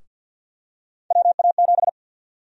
Morse code, heard as a keyed tone, about 51 wpm, 700 Hz. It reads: WA7
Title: USA Call Sign Prefixes - Repeated 2x - 50wpm